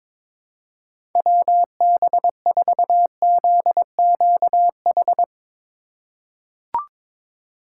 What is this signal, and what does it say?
Morse code, heard as a keyed tone, about 22 wpm, 700 Hz. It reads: WB4ZQH
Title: USA Call Signs - 3x Character Spacing - 22wpm